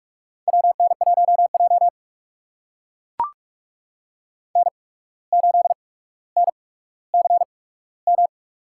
Morse code, heard as a keyed tone, about 44 wpm, spaced as for 18 wpm, 700 Hz. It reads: WN1J N8NCM
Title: USA Call Signs - 8x Character Spacing - 45wpm